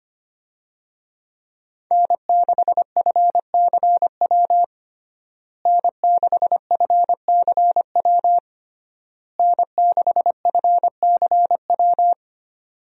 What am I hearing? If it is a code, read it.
Morse code, heard as a keyed tone, about 25 wpm, 700 Hz. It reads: N6FCW N6FCW N6FCW
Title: USA Call Signs - 5 Characters - Repeated 3x - 25wpm